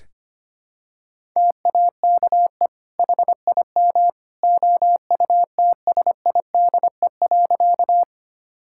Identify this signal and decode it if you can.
Morse code, heard as a keyed tone, about 25 wpm, 700 Hz. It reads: TAKE HIM OUTSIDE.
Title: Sentences from Top 1000 Words - Review Part 6 - 25wpm